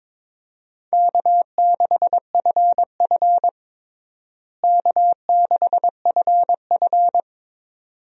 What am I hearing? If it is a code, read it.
Morse code, heard as a keyed tone, about 22 wpm, 700 Hz. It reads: K6FF K6FF
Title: USA Call Signs - 4 Characters - Repeated 2x - 22wpm